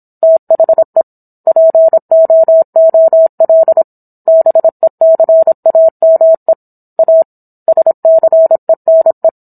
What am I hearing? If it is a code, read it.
Morse code, heard as a keyed tone, about 27 wpm, 650 Hz. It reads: THE POOL BECAME A SCENE